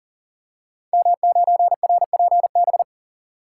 Morse code, heard as a keyed tone, about 42 wpm, 700 Hz. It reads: M9RPB
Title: Code Groups - 5 Letter:Numbers - 40wpm